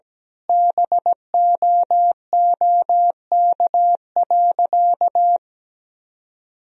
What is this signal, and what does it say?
Morse code, heard as a keyed tone, about 17 wpm, 700 Hz. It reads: BOOK.
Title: Sentences from Top 1000 Words - Review Part 1 - 17wpm